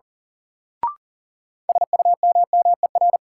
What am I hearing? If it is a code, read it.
Morse code, heard as a keyed tone, about 37 wpm, 700 Hz. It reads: SUMMER
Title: Sentences from Top 1100 Words - New Word Drill Alt - 40wpm